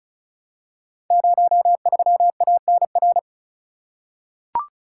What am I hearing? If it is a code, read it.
Morse code, heard as a keyed tone, about 36 wpm, 700 Hz. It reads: 03ANR
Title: Code Groups - 5 Letter:Numbers - 35wpm